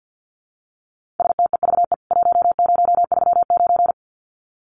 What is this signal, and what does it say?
Morse code, heard as a keyed tone, about 48 wpm, 700 Hz. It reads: STEVE 1039
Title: CWT Exchange 50wpm